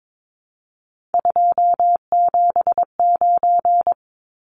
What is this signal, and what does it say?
Morse code, heard as a keyed tone, about 22 wpm, 700 Hz. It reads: 279
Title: Numbers - 3-Digit 22wpm